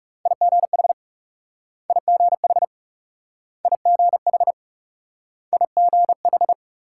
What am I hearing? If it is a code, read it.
Morse code, heard as a keyed tone, about 37 wpm, 700 Hz. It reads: IGH IGH IGH IGH
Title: Speed-Racing Multi-letter Phonemes - 30wpm